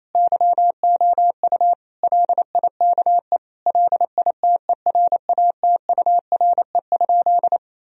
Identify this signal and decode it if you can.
Morse code, heard as a keyed tone, about 28 wpm, 700 Hz. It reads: YOU LIKE LITERATURE?
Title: Sentences from Top 1000 Words - Review Part 3 - 28wpm